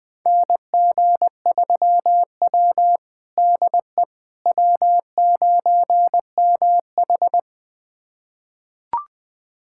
Morse code, heard as a keyed tone, about 20 wpm, 700 Hz. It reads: NG3W DE W9MH
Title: USA Callsign DE Callsign - 4 Characters - 20wpm